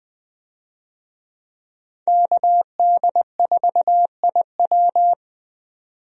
Morse code, heard as a keyed tone, about 20 wpm, 700 Hz. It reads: KD4IW
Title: USA Call Signs - 5 Characters - 20wpm